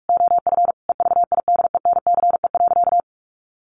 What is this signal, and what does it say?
Morse code, heard as a keyed tone, about 44 wpm, 700 Hz. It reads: OF EVIDENCE.
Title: Sentences from Top 1300 Words 45wpm